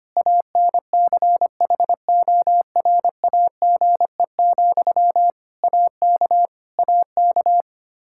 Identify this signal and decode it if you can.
Morse code, heard as a keyed tone, about 25 wpm, 700 Hz. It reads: ANCHORAGE, AK AK
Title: Practice QSO - Exchange 1 - Beginning - 25wpm